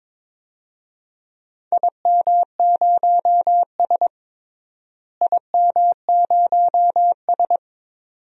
Morse code, heard as a keyed tone, about 22 wpm, 700 Hz. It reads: IM0S IM0S
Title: DXCC-100 Call Signs - 4 Characters - Repeated 2x - 22wpm